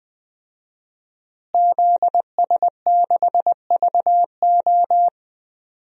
Morse code, heard as a keyed tone, about 20 wpm, 700 Hz. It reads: ZS6VO
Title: DXCC-100 Call Signs - 5 Characters - Repeated 2x - 20wpm